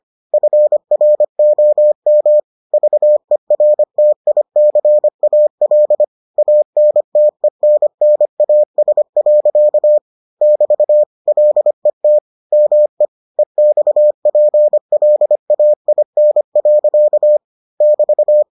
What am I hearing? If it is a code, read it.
Morse code, heard as a keyed tone, about 25 wpm, 600 Hz. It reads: FROM VERTICAL ANTENNAS. <BT> LET ME EXPLAIN. <BT>